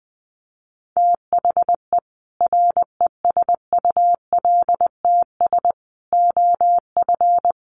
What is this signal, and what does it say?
Morse code, heard as a keyed tone, about 20 wpm, 700 Hz. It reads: THE RESULTS OF
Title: Sets of 5 Words 20wpm